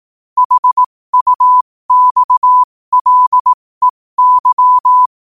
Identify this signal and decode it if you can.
Morse code, heard as a keyed tone, about 18 wpm, spaced as for 15 wpm, 1000 Hz. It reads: HUXLEY
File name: qotdfiles / 15wpm